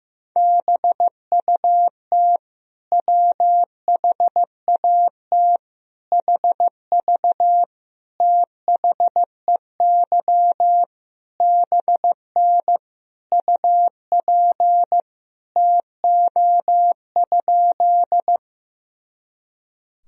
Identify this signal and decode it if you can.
Morse code, heard as a keyed tone, about 15 wpm, 700 Hz. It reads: BUT WHAT HV THEY BN UP TO?
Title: Sentences from Top 100 Words and Abbreviations 15wpm